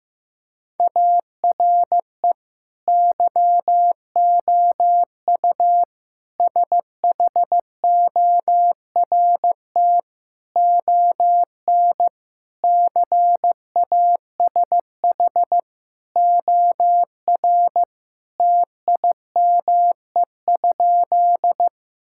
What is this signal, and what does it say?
Morse code, heard as a keyed tone, about 15 wpm, 700 Hz. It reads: ARE YOU SHORT ON CASH OR TIME?
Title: Sentences from Top 1000 Words - Review Part 5 - 15wpm